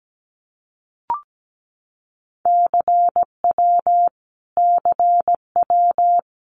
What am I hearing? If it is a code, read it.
Morse code, heard as a keyed tone, about 17 wpm, 700 Hz. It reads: CW CW
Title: Abbreviations Repeated 3x - 17wpm